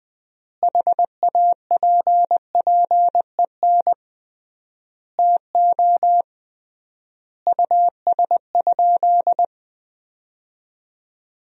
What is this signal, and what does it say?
Morse code, heard as a keyed tone, about 20 wpm, 700 Hz. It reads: HAPPEN TO US?
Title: 3x Word-Spacing - Sentences from Top 400 Words - 20wpm